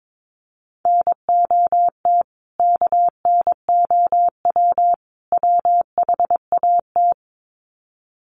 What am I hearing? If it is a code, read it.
Morse code, heard as a keyed tone, about 22 wpm, 700 Hz. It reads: NOT KNOW WHAT
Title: Sets of 3 Words 22wpm